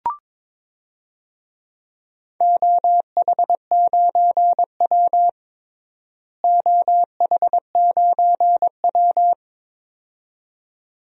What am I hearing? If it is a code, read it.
Morse code, heard as a keyed tone, about 22 wpm, 700 Hz. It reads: OH9W OH9W
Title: DXCC-100 Call Signs - 4 Characters - Repeated 2x - 22wpm